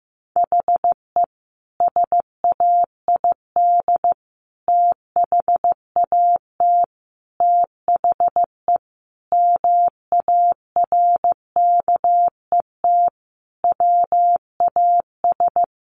Morse code, heard as a keyed tone, about 15 wpm, 700 Hz. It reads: HE SAID THAT THE MARKET WAS
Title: Sentences from Top 1400 Words 15wpm